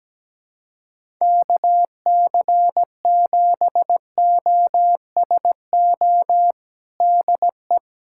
Morse code, heard as a keyed tone, about 17 wpm, 700 Hz. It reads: KC7OSO DE
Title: USA Callsign DE Callsign - 6 Characters - 17wpm